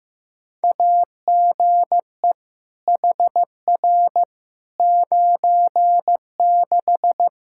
Morse code, heard as a keyed tone, about 15 wpm, 700 Hz. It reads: AGE HR 96
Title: K1USN SST QSO Practice 15wpm